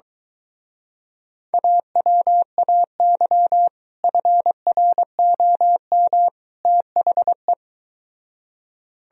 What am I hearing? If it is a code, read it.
Morse code, heard as a keyed tone, about 24 wpm, 700 Hz. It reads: AWAY FROM THE